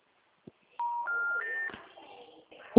Special information tone.